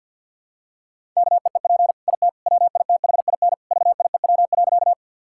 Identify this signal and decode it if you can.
Morse code, heard as a keyed tone, about 50 wpm, 700 Hz. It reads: KEEP IT WITHIN VIEW.
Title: Sentences from Top 1000 Words - Review Part 6 - 50wpm